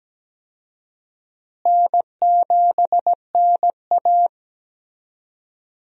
Morse code, heard as a keyed tone, about 17 wpm, 700 Hz. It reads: N7NA
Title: USA Call Signs - 4 Characters - Repeated 3x - 17wpm